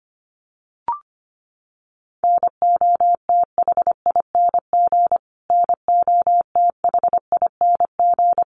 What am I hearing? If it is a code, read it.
Morse code, heard as a keyed tone, about 25 wpm, 700 Hz. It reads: NOTHING NOTHING
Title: Top 500 Words Repeated 25wpm